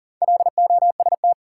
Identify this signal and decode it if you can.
Morse code, heard as a keyed tone, about 40 wpm, 700 Hz. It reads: LOST
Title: Sentences from Top 1400 Words - New Word Drill - 40wpm